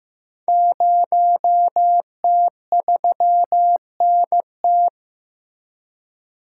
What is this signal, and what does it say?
Morse code, heard as a keyed tone, about 15 wpm, 700 Hz. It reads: 0T3NT
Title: Code Groups - 5 Letter:Numbers - 15wpm